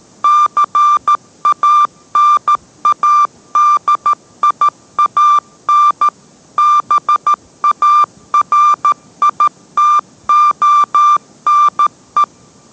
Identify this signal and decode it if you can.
Morse code, heard as a keyed tone, about 13 wpm, 1180 Hz. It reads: CANADIAN BARITONE